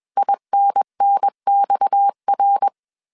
Morse code, heard as a keyed tone, about 21 wpm, 800 Hz. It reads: INNXR
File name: CW / DET / 1 6 1500C S B 20wpm 2048